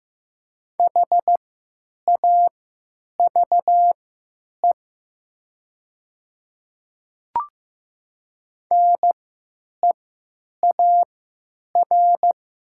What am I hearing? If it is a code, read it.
Morse code, heard as a keyed tone, about 15 wpm, spaced as for 9 wpm, 700 Hz. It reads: HAVE NEAR
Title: 3x Character Spacing - Top 300 Words - 15wpm